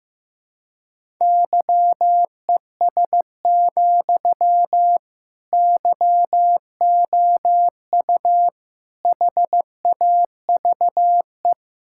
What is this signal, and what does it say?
Morse code, heard as a keyed tone, about 15 wpm, 700 Hz. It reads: YES, YOU HAVE
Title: Sentences from Top 1600 Words 15wpm